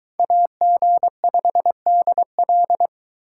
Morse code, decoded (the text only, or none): AG5DL